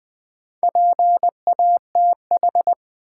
Morse code, heard as a keyed tone, about 20 wpm, 700 Hz. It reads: PATH
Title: Sentences from Top 1400 Words - New Word Drill - 20wpm